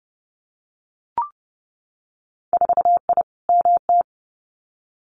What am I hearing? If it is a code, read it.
Morse code, heard as a keyed tone, about 30 wpm, 700 Hz. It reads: 4I MT
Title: Winter Field Day Exchanges Encore 30wpm